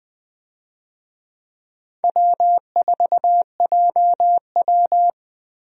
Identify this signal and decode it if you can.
Morse code, heard as a keyed tone, about 21 wpm, 700 Hz. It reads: W4JW
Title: USA Call Signs - 4 Characters - Repeated 3x - 20wpm